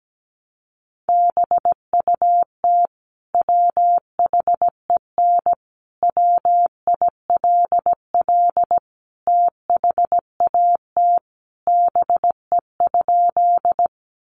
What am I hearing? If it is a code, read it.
Morse code, heard as a keyed tone, about 17 wpm, 700 Hz. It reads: BUT WHEN WILL THAT BE?
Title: Sentences from Top 100 Words 17wpm